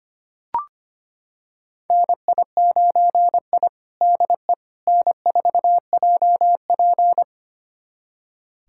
Morse code, heard as a keyed tone, about 25 wpm, 700 Hz. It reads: NI9I DE N4JP
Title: USA Callsign DE Callsign - 4 Characters - 25wpm